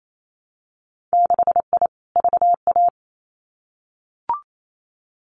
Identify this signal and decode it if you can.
Morse code, heard as a keyed tone, about 28 wpm, 690 Hz. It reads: 6I VA
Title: Winter Field Day Exchanges Encore 28wpm